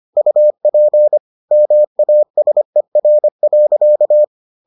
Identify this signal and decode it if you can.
Morse code, heard as a keyed tone, about 25 wpm, 600 Hz. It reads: UP MASER.